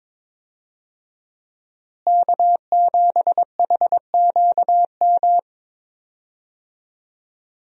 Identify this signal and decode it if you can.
Morse code, heard as a keyed tone, about 21 wpm, 700 Hz. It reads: K7HQM
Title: USA Call Signs - 5 Characters - Repeated 2x - 22wpm